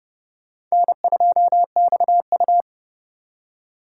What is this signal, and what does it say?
Morse code, heard as a keyed tone, about 37 wpm, 700 Hz. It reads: N2XU
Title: Speed Racing - USA 1x2 Call Signs 30wpm